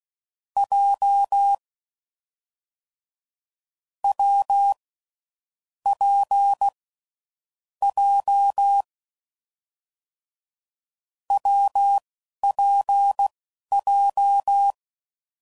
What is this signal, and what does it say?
Morse code, heard as a keyed tone, about 16 wpm, spaced as for 11 wpm, 800 Hz. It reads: J W P J WPJ
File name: Combinations / Difficult Combos / WPJ WPJ1